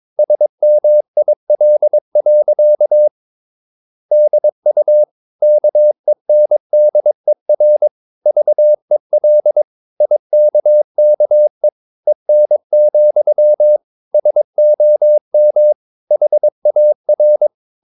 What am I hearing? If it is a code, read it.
Morse code, heard as a keyed tone, about 22 wpm, 600 Hz. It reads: SMIL. DU KENDER VEL IKKE EN, SOM HAR
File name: Ferie QRS 22wpm